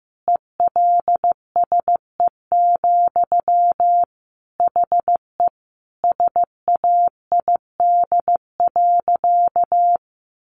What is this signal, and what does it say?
Morse code, heard as a keyed tone, about 15 wpm, 700 Hz. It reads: ELSE, HE SAID.
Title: Sentences from Top 800 Words 15wpm